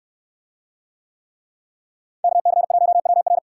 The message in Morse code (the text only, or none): KC1WR